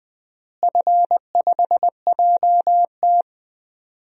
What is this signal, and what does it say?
Morse code, heard as a keyed tone, about 20 wpm, 700 Hz. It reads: F5JT
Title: DXCC-100 Call Signs - 4 Characters - Repeated 3x - 20wpm